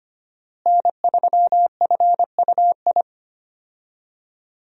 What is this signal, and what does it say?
Morse code, heard as a keyed tone, about 25 wpm, 700 Hz. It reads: N3FUI
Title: USA Call Signs - 5 Characters - Repeated 2x - 25wpm